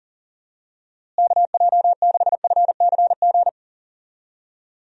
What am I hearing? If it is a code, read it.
Morse code, heard as a keyed tone, about 40 wpm, 700 Hz. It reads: KJ6FCG
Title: USA Call Signs - 6 Characters - Repeated 3x - 40wpm